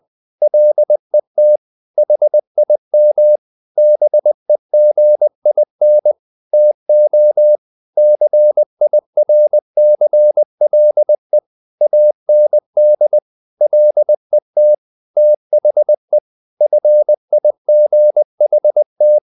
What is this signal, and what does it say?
Morse code, heard as a keyed tone, about 20 wpm, 600 Hz. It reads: LET HIM BEGIN TO CIRCLE AND LET THE FIGHT
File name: the old man and the sea part 2 20wpm